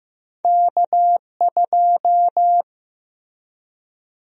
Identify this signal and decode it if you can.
Morse code, heard as a keyed tone, about 15 wpm, 700 Hz. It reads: K2
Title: USA Call Sign Prefixes - Repeated 2x - 15wpm